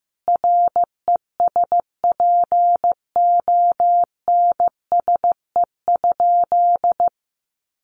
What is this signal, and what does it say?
Morse code, heard as a keyed tone, about 15 wpm, 700 Hz. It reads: RESPONSE?
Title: Sentences from Top 1400 Words 15wpm